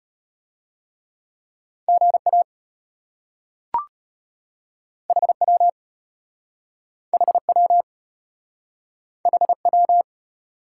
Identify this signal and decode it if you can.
Morse code, heard as a keyed tone, about 34 wpm, 700 Hz. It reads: GA HW HW HW